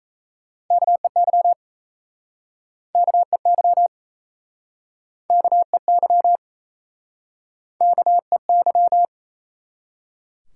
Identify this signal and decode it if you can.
Morse code, heard as a keyed tone, about 38 wpm, 700 Hz. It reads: KEY KEY KEY KEY